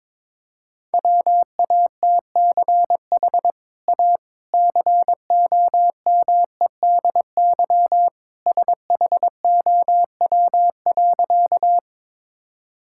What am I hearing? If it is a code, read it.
Morse code, heard as a keyed tone, about 22 wpm, 700 Hz. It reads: WATCH A COMEDY SHOW.
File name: practice / 22wpm / Easy Sentences with Hints - 22wpm